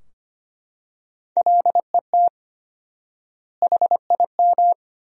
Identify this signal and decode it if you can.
Morse code, heard as a keyed tone, about 25 wpm, 700 Hz. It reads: LET HIM
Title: 4x Word-Spacing - Sentences from Top 300 Words - 25wpm